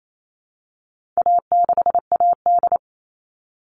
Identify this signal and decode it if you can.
Morse code, heard as a keyed tone, about 28 wpm, 700 Hz. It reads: A6AD
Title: All Cut Numbers - 4-Digit 28wpm